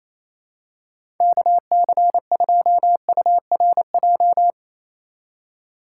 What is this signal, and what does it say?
Morse code, heard as a keyed tone, about 27 wpm, 700 Hz. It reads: KC2URJ
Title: USA Call Signs - 6 Characters - 3x Character Spacing - 28wpm